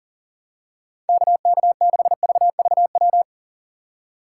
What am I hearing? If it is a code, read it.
Morse code, heard as a keyed tone, about 40 wpm, 700 Hz. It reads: KK6VVW
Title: USA Call Signs - 6 Characters - 6x Character Spacing - 40wpm